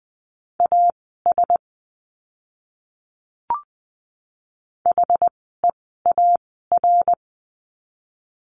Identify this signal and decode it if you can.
Morse code, heard as a keyed tone, about 20 wpm, spaced as for 14 wpm, 700 Hz. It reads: AS HEAR
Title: 230 Lesson 37f - Letter x - Words 20wpm